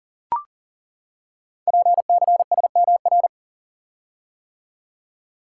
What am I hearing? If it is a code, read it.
Morse code, heard as a keyed tone, about 38 wpm, 690 Hz. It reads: PCSMR
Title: Code Groups - 5 Letter:Numbers - 40wpm